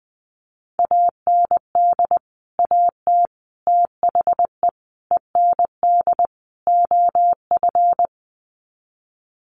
Morse code, heard as a keyed tone, about 19 wpm, 700 Hz. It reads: AND AT THE END OF
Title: Sets of 5 Words 20wpm